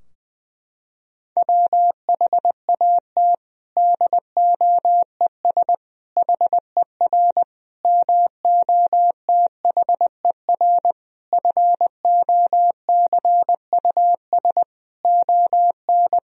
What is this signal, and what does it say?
Morse code, heard as a keyed tone, about 21 wpm, 700 Hz. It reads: WHAT DOES HER MOTHER FOCUS ON
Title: Sentences from Top 1000 Words - Review Part 4 - 20wpm